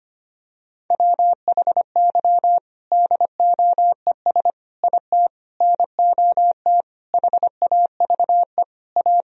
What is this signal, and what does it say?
Morse code, heard as a keyed tone, about 25 wpm, 700 Hz. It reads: WHY DOES IT NOT HAVE A
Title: Sentences from Top 1000 Words - Review Part 3 - 25wpm